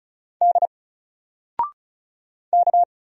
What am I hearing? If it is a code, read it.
Morse code, heard as a keyed tone, about 36 wpm, 700 Hz. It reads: D K
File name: Single-Letter-Number-35wpm